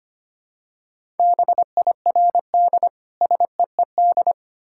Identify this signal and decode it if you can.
Morse code, heard as a keyed tone, about 25 wpm, 700 Hz. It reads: BIRD SEED
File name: practice / 25wpm / Difficult Words with Hints - 25wpm